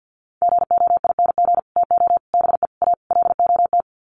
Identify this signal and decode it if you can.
Morse code, heard as a keyed tone, about 50 wpm, 700 Hz. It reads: GOING TO BE A LOT
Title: Sets of 5 Words Encore 50wpm